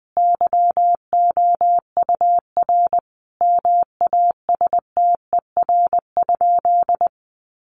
Morse code, heard as a keyed tone, about 20 wpm, 700 Hz. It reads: YOUR MASTER?
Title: Sentences from Top 1300 Words 20wpm